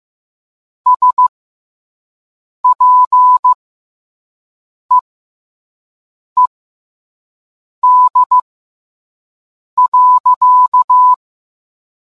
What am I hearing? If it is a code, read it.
Morse code, heard as a keyed tone, about 15 wpm, spaced as for 5 wpm, 1000 Hz. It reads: SPEED.